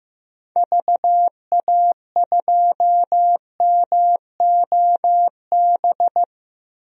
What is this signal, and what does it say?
Morse code, heard as a keyed tone, about 15 wpm, 700 Hz. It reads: VA2MOB
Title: DXCC-100 Call Signs - 6 Characters - Repeated 3x - 15wpm